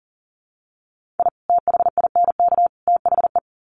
Morse code, heard as a keyed tone, about 40 wpm, 700 Hz. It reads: I THINK THE
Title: Sets of 3 Words 40wpm